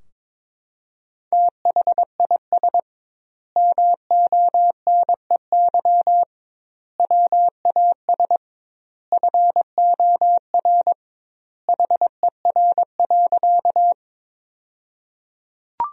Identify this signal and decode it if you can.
Morse code, heard as a keyed tone, about 22 wpm, 700 Hz. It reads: THIS MONEY WAS FOR HER.
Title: 2x Word-Spacing - Sentences from Top 400 Words - 22wpm